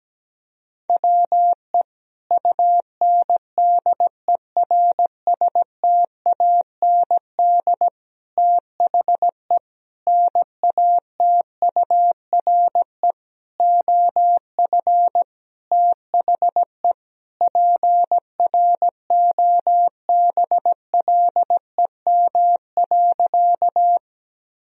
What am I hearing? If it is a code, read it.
Morse code, heard as a keyed tone, about 17 wpm, 700 Hz. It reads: WE UNDERSTAND THE NATURE OF THE PROBLEM.
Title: Sentences from Top 1000 Words - Review Part 2 - 17wpm